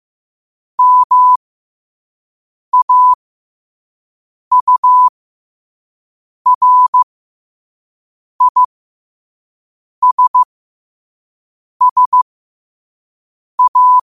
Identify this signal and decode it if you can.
Morse code, heard as a keyed tone, about 15 wpm, spaced as for 5 wpm, 1000 Hz. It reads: MAURISSA
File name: qotdfiles / 5wpm